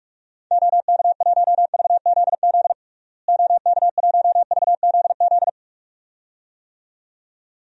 Morse code, heard as a keyed tone, about 47 wpm, 700 Hz. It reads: OK1VZZ OK1VZZ
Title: DXCC-100 Call Signs - 6 Characters - Repeated 2x - 45wpm